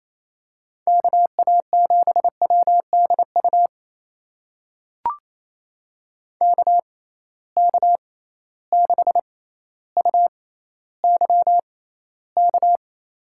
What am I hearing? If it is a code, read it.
Morse code, heard as a keyed tone, about 28 wpm, spaced as for 14 wpm, 700 Hz. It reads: KA7WDU KK6UYK
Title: USA Call Signs - 6 Characters - 6x Character Spacing - 28wpm